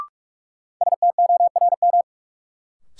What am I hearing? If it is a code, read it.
Morse code, heard as a keyed tone, about 47 wpm, 700 Hz. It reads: STORM
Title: Sentences from Top 1500 Words - New Word Drill Alt - 45wpm